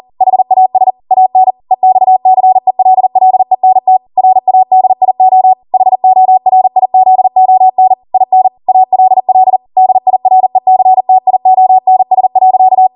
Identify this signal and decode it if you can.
Morse code, heard as a keyed tone, about 40 wpm, 750 Hz. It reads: HAS AN EXCELLENT RADIO HORIZON IN ALL DIRECTIONS.